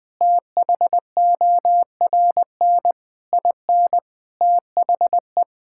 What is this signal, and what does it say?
Morse code, heard as a keyed tone, about 20 wpm, 700 Hz. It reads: THORN IN THE
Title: Sets of 8 Words - 20wpm